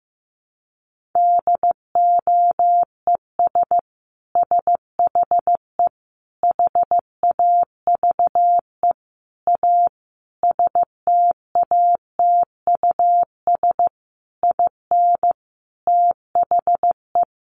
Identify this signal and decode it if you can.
Morse code, heard as a keyed tone, about 15 wpm, 700 Hz. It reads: DOES SHE HAVE A STATUS IN THE
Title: Sentences from Top 1400 Words 15wpm